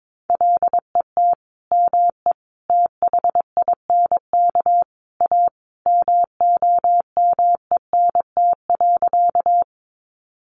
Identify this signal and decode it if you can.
Morse code, heard as a keyed tone, about 22 wpm, 700 Hz. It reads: LET ME THINK A MOMENT.
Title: Sentences from Top 1000 Words 22wpm